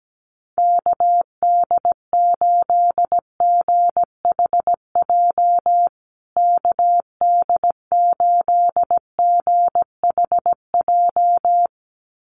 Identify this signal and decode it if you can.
Morse code, heard as a keyed tone, about 17 wpm, 700 Hz. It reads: KD8GHJ KD8GHJ
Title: USA Call Signs Repeated 2x 17wpm